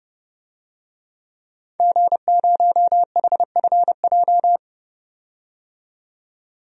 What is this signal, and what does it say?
Morse code, heard as a keyed tone, about 30 wpm, 700 Hz. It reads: G0HFJ
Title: DXCC-100 Call Signs - 5 Characters - 30wpm